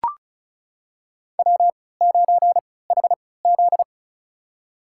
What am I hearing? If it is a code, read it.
Morse code, heard as a keyed tone, about 36 wpm, spaced as for 20 wpm, 700 Hz. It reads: W9HZ
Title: USA Call Signs - 4 Characters - 3x Character Spacing - 35wpm